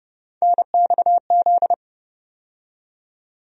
Morse code, heard as a keyed tone, about 29 wpm, 700 Hz. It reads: NXZ
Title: USA Call Sign Postfixes - 3 Letters 30wpm